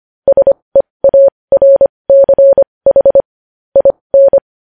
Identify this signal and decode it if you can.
Morse code, heard as a keyed tone, about 25 wpm, spaced as for 20 wpm, 550 Hz. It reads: SEARCH IN
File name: podcast / News headlines-1x-25x20